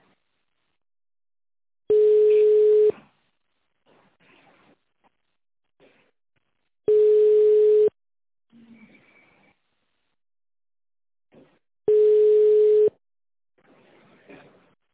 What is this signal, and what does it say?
Ringback tone (Europe, 425 Hz).